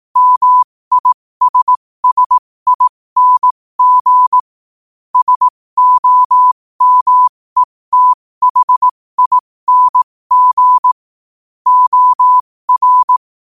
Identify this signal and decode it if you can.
Morse code, heard as a keyed tone, about 18 wpm, spaced as for 15 wpm, 1000 Hz. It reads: MISSING SOMETHING OR